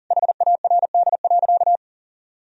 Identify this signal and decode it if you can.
Morse code, heard as a keyed tone, about 40 wpm, 700 Hz. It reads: HARD.
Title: Sentences from Top 1000 Words - Review Part 4 - 40wpm